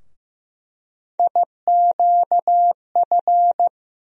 Morse code, heard as a keyed tone, about 15 wpm, 700 Hz. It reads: IQF
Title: USA Call Sign Postfixes - 3 Letters 15wpm